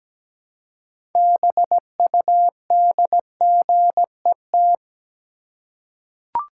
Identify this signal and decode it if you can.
Morse code, heard as a keyed tone, about 17 wpm, 700 Hz. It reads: BUDGET
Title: Six Letter Words - 17wpm